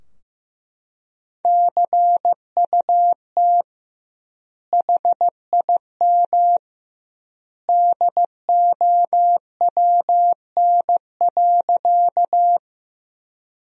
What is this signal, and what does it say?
Morse code, heard as a keyed tone, about 15 wpm, 700 Hz. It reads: CUT HIM DOWN.
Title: 2x Word-Spacing - Sentences from Top 300 Words - 15wpm